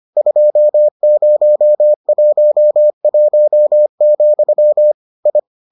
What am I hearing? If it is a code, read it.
Morse code, heard as a keyed tone, about 25 wpm, 600 Hz. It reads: 2011, I